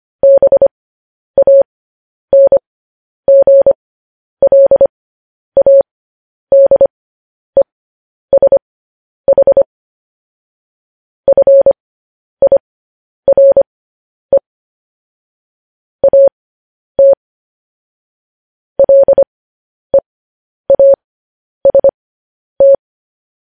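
Morse code, heard as a keyed tone, about 25 wpm, spaced as for 10 wpm, 550 Hz. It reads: BANGLADESH FIRE AT LEAST
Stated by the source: Morse news headlines at 10.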